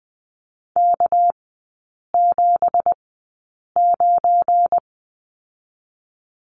Morse code, heard as a keyed tone, about 20 wpm, spaced as for 8 wpm, 690 Hz. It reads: K79
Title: 198 Lesson 32e - Letter k - Three Characters 20wpm